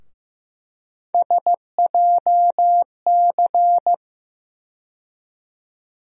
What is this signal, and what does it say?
Morse code, heard as a keyed tone, about 15 wpm, 700 Hz. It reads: SJC
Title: USA Call Sign Postfixes - 3 Letters 15wpm